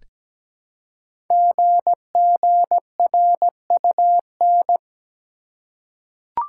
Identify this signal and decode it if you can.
Morse code, heard as a keyed tone, about 17 wpm, 700 Hz. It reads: GGRUN
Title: Code Groups - 5 Letters - 17wpm